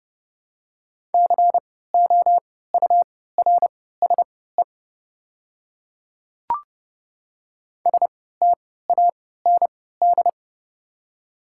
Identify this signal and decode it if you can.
Morse code, heard as a keyed tone, about 29 wpm, spaced as for 17 wpm, 700 Hz. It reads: COURSE STAND
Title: 3x Character Spacing - Top 500 Words - 30wpm